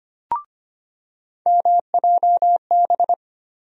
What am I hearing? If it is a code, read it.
Morse code, heard as a keyed tone, about 25 wpm, 700 Hz. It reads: MJB
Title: USA Call Sign Postfixes - 3 Letters 25wpm